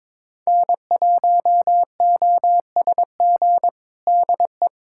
Morse code, heard as a keyed tone, about 21 wpm, 700 Hz. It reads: N1OSG DE
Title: USA Callsign DE Callsign - 5 Characters - 22wpm